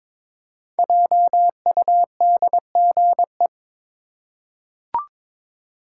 Morse code, heard as a keyed tone, about 22 wpm, 700 Hz. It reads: JUDGE
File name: 5 Letter Words - First 3 Letters Given - 22wpm